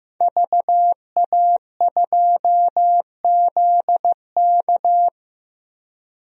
Morse code, heard as a keyed tone, about 16 wpm, 700 Hz. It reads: VA2ZK
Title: DXCC-100 Call Signs - 5 Characters - 15wpm